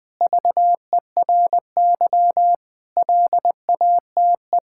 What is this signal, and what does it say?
Morse code, heard as a keyed tone, about 21 wpm, 700 Hz. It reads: VERY LATE
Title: Sentences from Top 1000 Words - Review Part 6 - 20wpm